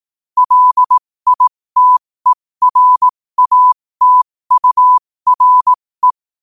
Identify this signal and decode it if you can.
Morse code, heard as a keyed tone, about 18 wpm, spaced as for 15 wpm, 1000 Hz. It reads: LITERATURE
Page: New quotes every day in morse code at 15 Words per minute.